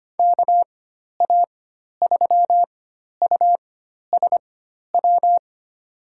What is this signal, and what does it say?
Morse code, heard as a keyed tone, about 25 wpm, spaced as for 12 wpm, 700 Hz. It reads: KA3USW
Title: USA Call Signs - 6 Characters - 4x Character Spacing - 25wpm